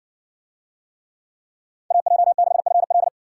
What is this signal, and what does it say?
Morse code, heard as a keyed tone, about 57 wpm, 700 Hz. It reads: AJ6FL